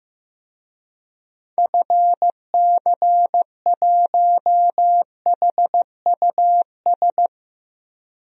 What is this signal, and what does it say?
Morse code, heard as a keyed tone, about 16 wpm, 700 Hz. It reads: FC1HUS
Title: DXCC-100 Call Signs - 6 Characters - 15wpm